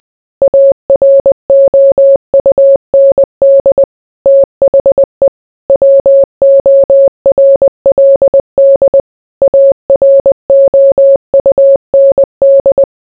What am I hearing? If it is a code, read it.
Morse code, heard as a keyed tone, about 20 wpm, 550 Hz. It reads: AROUND THE WORLD AROUND